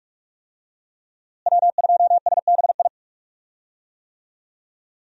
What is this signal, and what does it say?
Morse code, heard as a keyed tone, about 43 wpm, 700 Hz. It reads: W2SBI
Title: USA Call Signs - 5 Characters - Repeated 3x - 45wpm